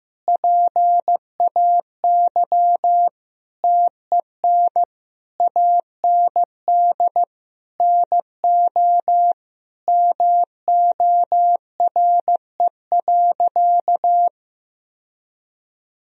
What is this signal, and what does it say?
Morse code, heard as a keyed tone, about 15 wpm, 700 Hz. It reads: PAY TEN AND NO MORE.
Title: Sentences from Top 1000 Words - Review Part 2 - 15wpm